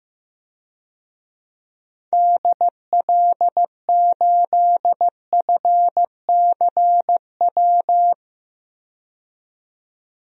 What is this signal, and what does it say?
Morse code, heard as a keyed tone, about 16 wpm, 700 Hz. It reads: DL8FCW
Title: DXCC-100 Call Signs - 6 Characters - 15wpm